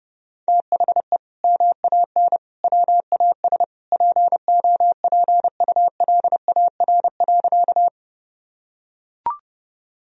Morse code, heard as a keyed tone, about 30 wpm, 700 Hz. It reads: THE MAN WAS POPULAR.
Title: Sentences from Top 1000 Words - Review Part 3 - 30wpm